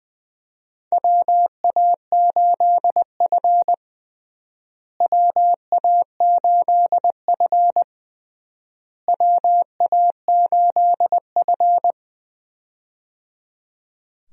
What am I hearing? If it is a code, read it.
Morse code, heard as a keyed tone, about 20 wpm, 700 Hz. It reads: WA8F WA8F WA8F
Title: USA Call Signs - 4 Characters - Repeated 3x - 20wpm